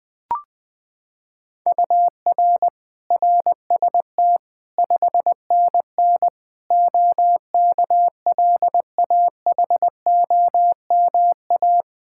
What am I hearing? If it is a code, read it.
Morse code, heard as a keyed tone, about 20 wpm, 700 Hz. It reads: UR RST 5NN OKLAHOMA
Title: Practice QSO - Exchange 1 - Beginning - 20wpm